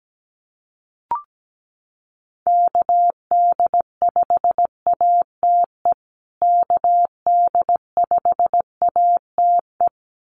Morse code, heard as a keyed tone, about 17 wpm, 700 Hz. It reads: KD5ATE KD5ATE
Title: USA Call Signs Repeated 2x 17wpm